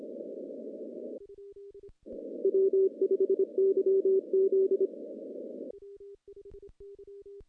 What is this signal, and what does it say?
Morse code, heard as a keyed tone, about 23 wpm, 380 Hz. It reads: W5YZ
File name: contests / CW-SS-k5tr- -1400